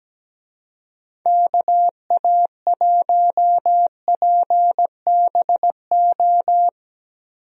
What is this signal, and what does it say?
Morse code, heard as a keyed tone, about 17 wpm, 700 Hz. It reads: KA1PBO
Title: USA Call Signs - 6 Characters - Repeated 2x - 17wpm